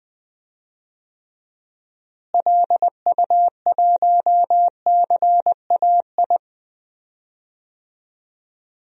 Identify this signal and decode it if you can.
Morse code, heard as a keyed tone, about 21 wpm, 700 Hz. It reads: LU1CAI
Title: DXCC-100 Call Signs - 6 Characters - Repeated 3x - 20wpm